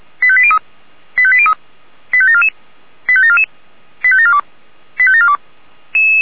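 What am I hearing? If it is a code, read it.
Five-tone selective call (ZVEI), heard as quick runs of five tones, tones of about 70 ms. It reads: 85803 85803 85744 85744 85742 85742
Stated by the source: ZVEI five-tone selective call 85730 received on an SDR.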